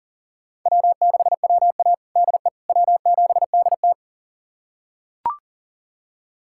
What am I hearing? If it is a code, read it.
Morse code, heard as a keyed tone, about 40 wpm, 700 Hz. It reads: W6WA DE W7DT
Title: USA Callsign DE Callsign - 4 Characters - 40wpm